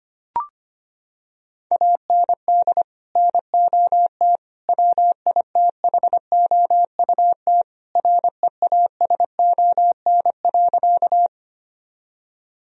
Morse code, heard as a keyed tone, about 25 wpm, 700 Hz. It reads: AND NOT WITHOUT REASON.
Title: Sentences from Top 1000 Words - Review Part 3 - 25wpm